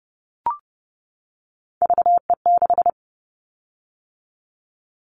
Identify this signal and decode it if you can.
Morse code, heard as a keyed tone, about 30 wpm, 700 Hz. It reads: VE6
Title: Callsign Prefixes for Top 25 DXCC - 30wpm